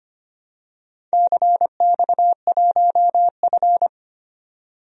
Morse code, heard as a keyed tone, about 25 wpm, 700 Hz. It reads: CX1F
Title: DXCC-100 Call Signs - 4 Characters - 25wpm